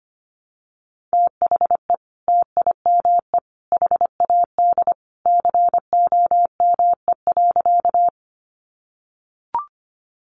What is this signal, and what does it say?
Morse code, heard as a keyed tone, about 25 wpm, 700 Hz. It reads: THE TIME HAD COME.
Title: Sentences from Top 100 Words 25wpm